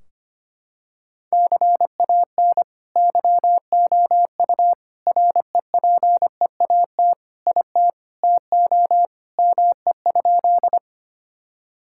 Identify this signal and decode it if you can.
Morse code, heard as a keyed tone, about 25 wpm, 700 Hz. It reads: CAN YOU REPEAT IT TO ME?
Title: Sentences from Top 1600 Words 25wpm